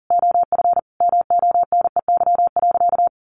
Morse code, heard as a keyed tone, about 40 wpm, 700 Hz. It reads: OF MONEY.
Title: Easy Phrases 40wpm